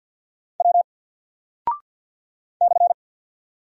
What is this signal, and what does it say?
Morse code, heard as a keyed tone, about 50 wpm, 700 Hz. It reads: W /
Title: Single Character (Letter:Number:Punctuation:Prosign) 50wpm